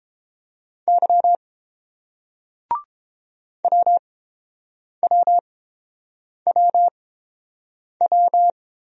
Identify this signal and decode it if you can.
Morse code, heard as a keyed tone, about 30 wpm, spaced as for 7 wpm, 700 Hz. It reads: Y WWWW